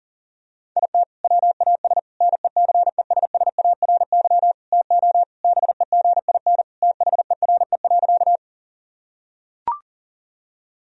Morse code, heard as a keyed tone, about 40 wpm, 700 Hz. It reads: IT WAS NECESSARY TO BEGIN THERE.
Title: Sentences from Top 1000 Words - Review Part 1 - 40wpm